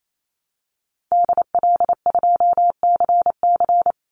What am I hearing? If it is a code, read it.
Morse code, heard as a keyed tone, about 28 wpm, 700 Hz. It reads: DL2CC
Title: Call Signs Repeated 2x - 28wpm